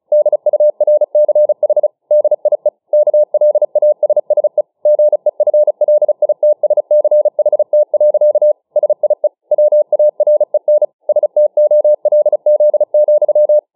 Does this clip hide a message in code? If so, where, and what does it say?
Yes: Morse code from 0.1 to 13.6 s, heard as a keyed tone, about 35 wpm, 600 Hz. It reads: DURCH DIE KLASSE GEFLITSCHT. SIE WAREN STOLZ,